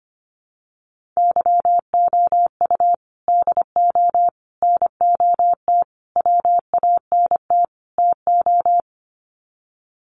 Morse code, heard as a keyed tone, about 25 wpm, 700 Hz. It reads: YOU DO NOT WANT TO
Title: Sets of 5 Words 25wpm